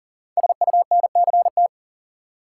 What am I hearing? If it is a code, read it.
Morse code, heard as a keyed tone, about 39 wpm, 700 Hz. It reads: SUNCT
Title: Code Groups - 5 Letter:Numbers - 40wpm